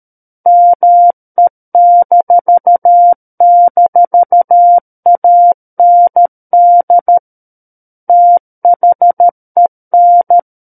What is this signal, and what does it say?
Morse code, heard as a keyed tone, about 13 wpm, 700 Hz. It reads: ME--AND THEN
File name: Book One / 04-Chapter 4 - 13 WPM